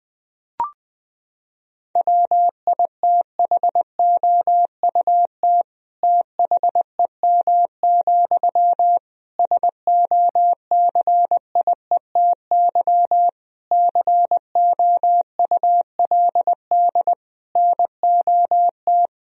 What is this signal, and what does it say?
Morse code, heard as a keyed tone, about 19 wpm, 700 Hz. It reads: WITHOUT THEM, SOCIETY COULD NOT
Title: Sentences from Top 1000 Words - Review Part 4 - 20wpm